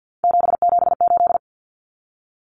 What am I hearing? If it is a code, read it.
Morse code, heard as a keyed tone, about 50 wpm, 700 Hz. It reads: 778
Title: Numbers - 3-Digit 50wpm